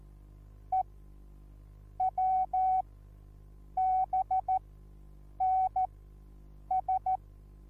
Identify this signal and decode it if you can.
Morse code, heard as a keyed tone, about 14 wpm, spaced as for 7 wpm, 750 Hz. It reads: EWBNS
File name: Onlinelehrgang / morsen / DJ4UF Morsen21